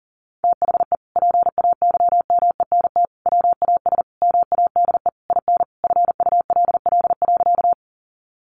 Morse code, heard as a keyed tone, about 37 wpm, 700 Hz. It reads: THE PAYMENT WAS MADE IN FULL.
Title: Sentences from Top 1200 Words 40wpm